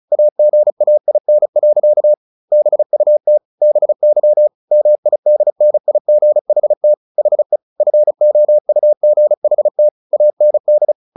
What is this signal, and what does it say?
Morse code, heard as a keyed tone, about 34 wpm, 600 Hz. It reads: AGAIN. BUT BY MIDNIGHT HE FOUGHT AND
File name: the old man and the sea 2 35wpm